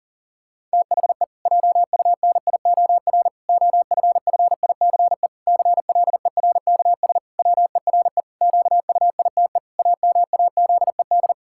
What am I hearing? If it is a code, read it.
Morse code, heard as a keyed tone, about 40 wpm, 700 Hz. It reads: THE JUNIOR OFFICE CLERKS WERE QUITE AMAZED
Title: Pangram Challenge v2 - 40wpm